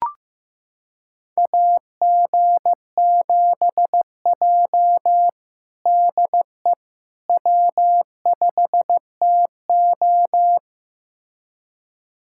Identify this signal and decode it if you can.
Morse code, heard as a keyed tone, about 15 wpm, 700 Hz. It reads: AG7J DE W5TO
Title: USA Callsign DE Callsign - 4 Characters - 15wpm